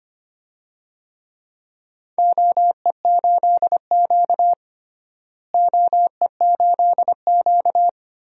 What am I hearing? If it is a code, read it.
Morse code, heard as a keyed tone, about 25 wpm, 700 Hz. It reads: OE8Q OE8Q
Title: DXCC-100 Call Signs - 4 Characters - Repeated 3x - 25wpm